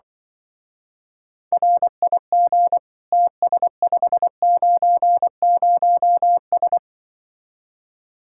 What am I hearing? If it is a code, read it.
Morse code, heard as a keyed tone, about 24 wpm, 700 Hz. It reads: RIG TS590S
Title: Speed-Racing Common QSO Phrases - 20wpm